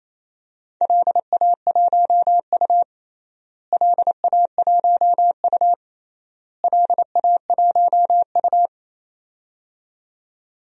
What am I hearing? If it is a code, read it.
Morse code, heard as a keyed tone, about 28 wpm, 700 Hz. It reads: LA1U LA1U LA1U
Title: DXCC-100 Call Signs - 4 Characters - Repeated 3x - 28wpm